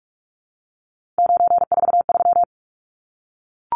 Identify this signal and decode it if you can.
Morse code, heard as a keyed tone, about 44 wpm, 700 Hz. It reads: 943
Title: Numbers - 3-Digit 45wpm